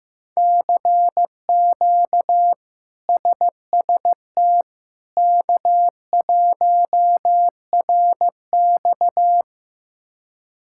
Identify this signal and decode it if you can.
Morse code, heard as a keyed tone, about 15 wpm, 700 Hz. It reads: CQ SST K1RX
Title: K1USN SST QSO Practice - 15wpm